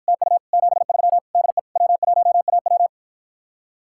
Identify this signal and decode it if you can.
Morse code, heard as a keyed tone, about 53 wpm, 700 Hz. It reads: TU 73 DE W1AW